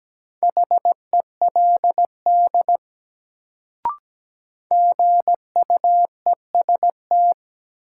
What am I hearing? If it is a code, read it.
Morse code, heard as a keyed tone, about 17 wpm, 700 Hz. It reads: HELD GUEST
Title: Sentences from Top 1600 Words - New Word Drill Alt - 17wpm